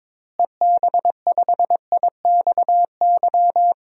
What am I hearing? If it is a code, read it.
Morse code, heard as a keyed tone, about 22 wpm, 700 Hz. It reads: EB5IXY
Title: DXCC-100 Call Signs - 6 Characters - Repeated 3x - 22wpm